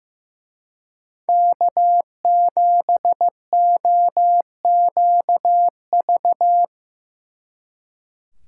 Morse code, heard as a keyed tone, about 17 wpm, 700 Hz. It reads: K7OQV
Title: Speed Racing - USA 1x3 Call Signs 15wpm